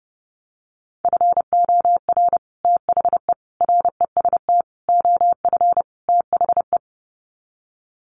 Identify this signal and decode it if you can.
Morse code, heard as a keyed tone, about 30 wpm, 700 Hz. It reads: FOR THE REST OF THE
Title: Sets of 5 Words 30wpm